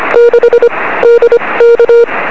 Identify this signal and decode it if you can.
Morse code, heard as a keyed tone, about 25 wpm, spaced as for 20 wpm, 500 Hz. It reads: 6 DK
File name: 0544z yl2ao 23cm